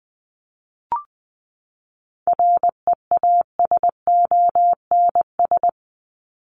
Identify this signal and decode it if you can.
Morse code, heard as a keyed tone, about 20 wpm, 700 Hz. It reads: REASONS
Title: ES:S Suffix - Top 100 Words with Base Words 20wpm